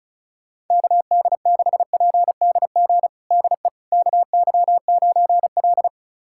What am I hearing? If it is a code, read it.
Morse code, heard as a keyed tone, about 36 wpm, 700 Hz. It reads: KD6PDG DE KY9L
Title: USA Callsign DE Callsign 35wpm